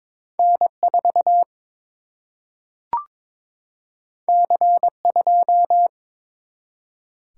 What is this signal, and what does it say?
Morse code, heard as a keyed tone, about 22 wpm, 700 Hz. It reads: N 4 C 2
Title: Single-Letter Single-Number 22wpm